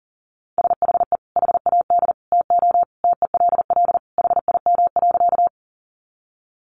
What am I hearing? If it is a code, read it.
Morse code, heard as a keyed tone, about 37 wpm, 700 Hz. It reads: SHE HAD TO TELL HIM.
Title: Sentences from Top 200 Words 40wpm